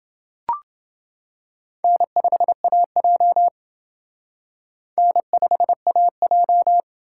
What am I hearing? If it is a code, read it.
Morse code, heard as a keyed tone, about 28 wpm, 700 Hz. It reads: N5AJ N5AJ